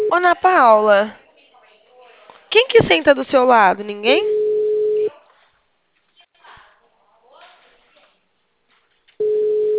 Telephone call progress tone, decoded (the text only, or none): ringback tone